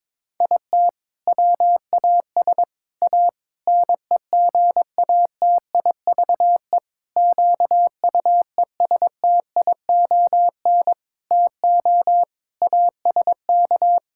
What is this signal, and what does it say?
Morse code, heard as a keyed tone, about 22 wpm, 700 Hz. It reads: IT WAS A NEGATIVE QUESTION TO ASK
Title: Sentences from Top 1000 Words - Review Part 6 - 22wpm